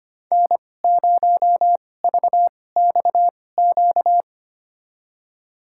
Morse code, heard as a keyed tone, about 25 wpm, spaced as for 18 wpm, 700 Hz. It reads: N0VXQ
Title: USA Call Signs - 5 Characters - 2x Character Spacing - 25wpm